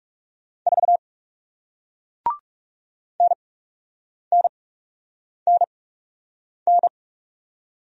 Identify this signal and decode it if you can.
Morse code, heard as a keyed tone, about 40 wpm, spaced as for 8 wpm, 700 Hz. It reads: 4 NNNN